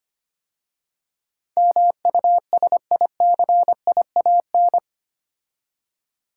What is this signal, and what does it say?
Morse code, heard as a keyed tone, about 25 wpm, 700 Hz. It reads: MUSICIAN
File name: 8 Letter Words - First 3 Letters Given - 25wpm